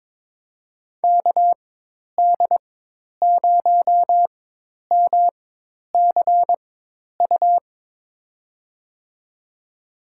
Morse code, heard as a keyed tone, about 22 wpm, spaced as for 10 wpm, 700 Hz. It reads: KD0MCU
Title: USA Call Signs - 6 Characters - 4x Character Spacing - 22wpm